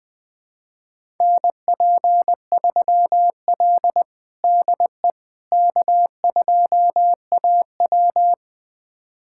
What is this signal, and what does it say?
Morse code, heard as a keyed tone, about 21 wpm, 700 Hz. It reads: NP3L DE K2AW
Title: USA Callsign DE Callsign - 4 Characters - 20wpm